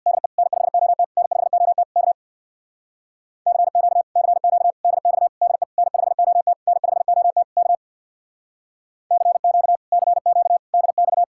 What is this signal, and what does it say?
Morse code, heard as a keyed tone, about 62 wpm, 700 Hz. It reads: DE N5OT N5OT K CQ CQ DX DE N5OT N5OT K CQ CQ DX
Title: Speed-Racing Common QSO Phrases - 50wpm